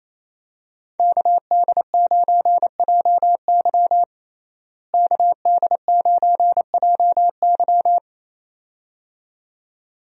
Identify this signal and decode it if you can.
Morse code, heard as a keyed tone, about 28 wpm, 700 Hz. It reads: KD9JY KD9JY
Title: USA Call Signs - 5 Characters - Repeated 2x - 28wpm